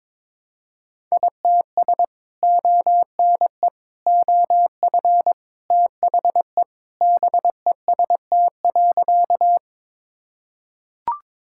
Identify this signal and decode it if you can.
Morse code, heard as a keyed tone, about 22 wpm, 700 Hz. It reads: ITS ONE OF THE BEST.
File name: practice / 22wpm / Sentences with 3-Word Phrases - 22wpm